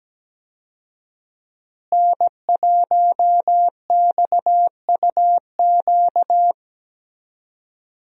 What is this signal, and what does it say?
Morse code, heard as a keyed tone, about 17 wpm, 700 Hz. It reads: N1XUQ
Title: USA Call Signs - 5 Characters - Repeated 3x - 17wpm